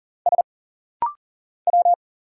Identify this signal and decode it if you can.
Morse code, heard as a keyed tone, about 40 wpm, 700 Hz. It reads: S W
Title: Single Letters - Rapid-Fire 40wpm